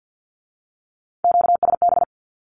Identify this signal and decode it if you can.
Morse code, heard as a keyed tone, about 50 wpm, 700 Hz. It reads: QSB
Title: Q-Codes with Spoken Meaning - 50wpm